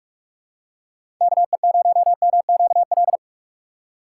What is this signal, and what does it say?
Morse code, heard as a keyed tone, about 46 wpm, 700 Hz. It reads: KE0MQL
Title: USA Call Signs - 6 Characters - 4x Character Spacing - 45wpm